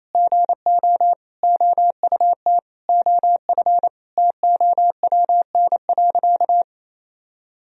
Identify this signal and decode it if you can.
Morse code, heard as a keyed tone, about 28 wpm, 700 Hz. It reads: GO OUT OF TOWN.
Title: Sentences from Top 1000 Words - Review Part 4 - 28wpm